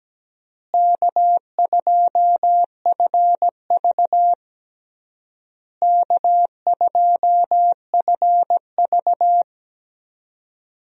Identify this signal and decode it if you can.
Morse code, heard as a keyed tone, about 17 wpm, 700 Hz. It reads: K2FV K2FV
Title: USA Call Signs - 4 Characters - Repeated 3x - 17wpm